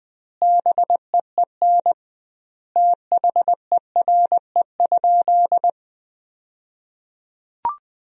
Morse code, heard as a keyed tone, about 21 wpm, 700 Hz. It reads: BEEN THERE?
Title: 2x Word-Spacing - Sentences from Top 100 Words - 20wpm